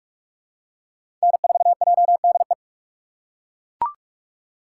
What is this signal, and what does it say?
Morse code, heard as a keyed tone, about 42 wpm, 700 Hz. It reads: N4JDE
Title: USA Call Signs - 5 Characters - 3x Character Spacing - 45wpm